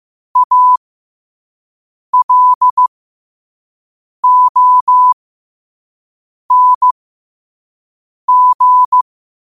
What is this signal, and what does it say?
Morse code, heard as a keyed tone, about 15 wpm, spaced as for 5 wpm, 1000 Hz. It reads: ALONG
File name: qotdfiles / 5wpm